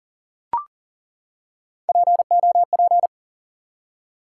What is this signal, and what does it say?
Morse code, heard as a keyed tone, about 39 wpm, 700 Hz. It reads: POP
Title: Three Letter Words - 40wpm